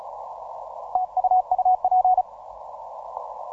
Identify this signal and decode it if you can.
Morse code, heard as a keyed tone, about 36 wpm, 740 Hz. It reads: TU UP
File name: simple-file-list / 9N7AA 12MCW